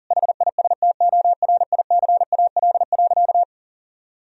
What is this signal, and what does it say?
Morse code, heard as a keyed tone, about 40 wpm, 700 Hz. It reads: HISTORICAL.
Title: Sentences from Top 1000 Words - Review Part 4 - 40wpm